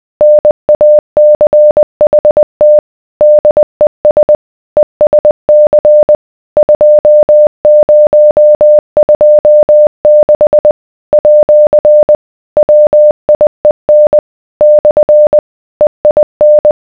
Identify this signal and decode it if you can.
Morse code, heard as a keyed tone, about 20 wpm, 600 Hz. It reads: NACHT DES ESC 2026 @ WIEN / EIN